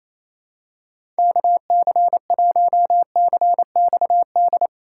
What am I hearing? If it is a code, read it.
Morse code, heard as a keyed tone, about 28 wpm, 700 Hz. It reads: KC1CXD
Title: USA Call Signs - 6 Characters - 3x Character Spacing - 28wpm